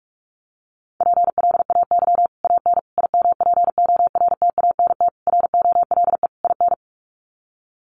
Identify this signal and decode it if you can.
Morse code, heard as a keyed tone, about 44 wpm, 700 Hz. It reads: PLAY AN IMPORTANT ROLE IN
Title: Sets of 5 Words Encore 45wpm